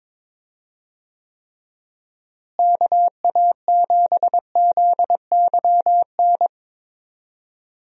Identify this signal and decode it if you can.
Morse code, heard as a keyed tone, about 22 wpm, 700 Hz. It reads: KA7ZYN
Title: USA Call Signs - 6 Characters - Repeated 2x - 22wpm